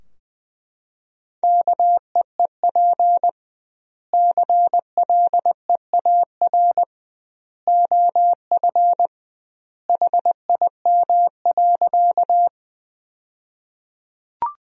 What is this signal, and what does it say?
Morse code, heard as a keyed tone, about 21 wpm, 700 Hz. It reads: KEEP CLEAR OF HIM.
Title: 2x Word-Spacing - Sentences from Top 500 Words - 20wpm